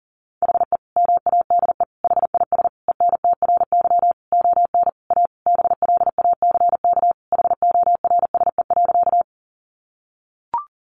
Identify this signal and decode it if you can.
Morse code, heard as a keyed tone, about 40 wpm, 700 Hz. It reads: HE MADE HIS ENTRY ON A BLACK HORSE.
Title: Sentences from Top 1500 Words 40wpm